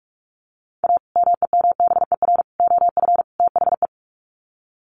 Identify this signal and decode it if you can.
Morse code, heard as a keyed tone, about 44 wpm, 700 Hz. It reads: A MEMBER OF THE
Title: Sets of 4 Words 45wpm